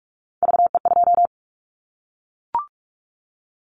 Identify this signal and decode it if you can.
Morse code, heard as a keyed tone, about 44 wpm, 700 Hz. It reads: VE2
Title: Callsign Prefixes for Top 100 DXCC - 45wpm